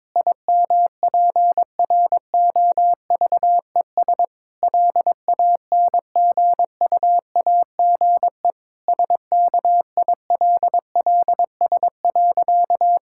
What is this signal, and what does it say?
Morse code, heard as a keyed tone, about 22 wpm, 700 Hz. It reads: IMPROVES LANGUAGE SKILLS.
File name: Difficult Sentences with Hints - 22wpm